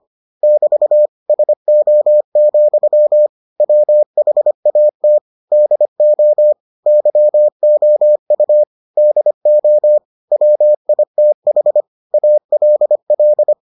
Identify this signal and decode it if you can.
Morse code, heard as a keyed tone, about 25 wpm, 600 Hz. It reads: <BT> SO, WHAT DO YOU DO WITH ALL